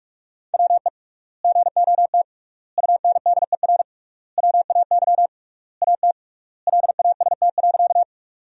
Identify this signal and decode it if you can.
Morse code, heard as a keyed tone, about 43 wpm, 700 Hz. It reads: WE GOT UNDER WAY AT LAST.
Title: 3x Word-Spacing - Sentences from Top 300 Words - 45wpm